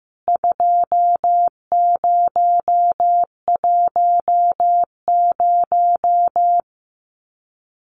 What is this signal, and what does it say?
Morse code, heard as a keyed tone, about 14 wpm, 700 Hz. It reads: 2010
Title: CWT Exchange 15wpm